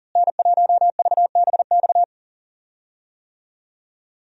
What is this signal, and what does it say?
Morse code, heard as a keyed tone, about 40 wpm, 700 Hz. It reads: N1VBX
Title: USA Call Signs - 5 Characters - Repeated 2x - 40wpm